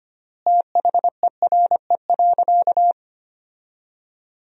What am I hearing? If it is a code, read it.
Morse code, heard as a keyed tone, about 25 wpm, 700 Hz. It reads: THERE.
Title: 4x Word-Spacing - Sentences from Top 500 Words - 25wpm